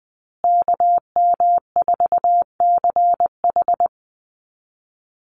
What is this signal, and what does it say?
Morse code, heard as a keyed tone, about 20 wpm, 700 Hz. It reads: KM4CH
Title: Call Signs Repeated 2x - 20wpm